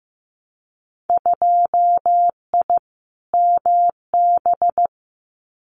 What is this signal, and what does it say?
Morse code, heard as a keyed tone, about 15 wpm, 700 Hz. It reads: 2I MB
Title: Winter Field Day Exchanges Encore 15wpm